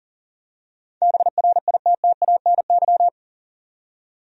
Morse code, heard as a keyed tone, about 40 wpm, 700 Hz. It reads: BRITTANY
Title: USA Common Names - 40wpm